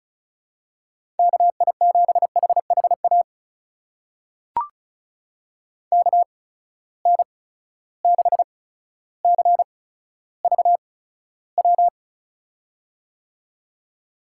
Morse code, heard as a keyed tone, about 34 wpm, spaced as for 15 wpm, 700 Hz. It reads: KI7HHA KN6CVW
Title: USA Call Signs - 6 Characters - 8x Character Spacing - 35wpm